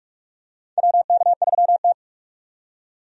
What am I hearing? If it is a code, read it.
Morse code, heard as a keyed tone, about 46 wpm, 700 Hz. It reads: WK3T
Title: USA Call Signs - 4 Characters - 45wpm